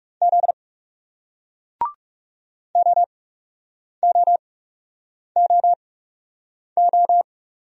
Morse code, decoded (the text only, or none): Z OOOO